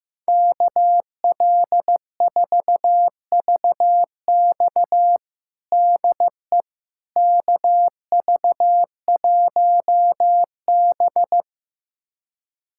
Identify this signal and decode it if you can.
Morse code, heard as a keyed tone, about 15 wpm, 700 Hz. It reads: KL4VX DE KV1B
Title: USA Callsign DE Callsign 15wpm